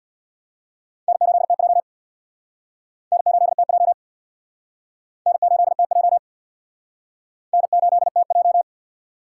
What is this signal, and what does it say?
Morse code, heard as a keyed tone, about 66 wpm, 700 Hz. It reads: N8TJ N8TJ N8TJ N8TJ